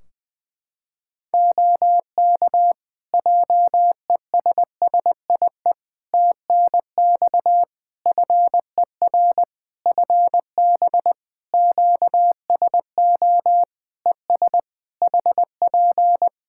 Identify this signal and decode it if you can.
Morse code, heard as a keyed tone, about 20 wpm, 700 Hz. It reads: OK JESSIE TNX FER FB QSO ES HP
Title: Practice QSO - Exchange 4 - Ending - 20wpm